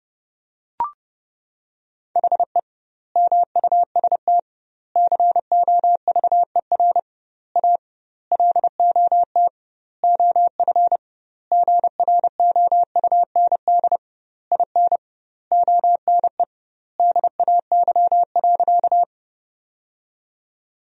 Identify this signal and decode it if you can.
Morse code, heard as a keyed tone, about 30 wpm, 700 Hz. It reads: HE MUST COVER A LOT OF GROUND IN ONE DAY.
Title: 2x Word-Spacing - Sentences from Top 500 Words - 30wpm